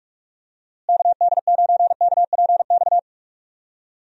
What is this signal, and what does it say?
Morse code, heard as a keyed tone, about 46 wpm, 700 Hz. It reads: KD9KPX
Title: USA Call Signs - 6 Characters - Repeated 3x - 45wpm